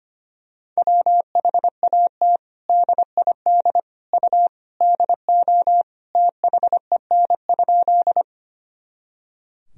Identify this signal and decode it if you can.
Morse code, heard as a keyed tone, about 25 wpm, 700 Hz. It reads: WHAT DID U DO THEN?
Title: Sentences from Top 100 Words and Abbreviations 25wpm